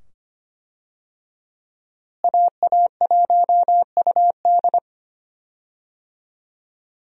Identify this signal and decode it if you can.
Morse code, heard as a keyed tone, about 25 wpm, 700 Hz. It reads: AA1UD
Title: USA Call Signs - 5 Characters - Repeated 2x - 25wpm